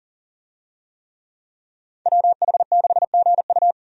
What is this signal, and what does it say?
Morse code, heard as a keyed tone, about 37 wpm, 700 Hz. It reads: WH6GU
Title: USA Call Signs - 5 Characters - 40wpm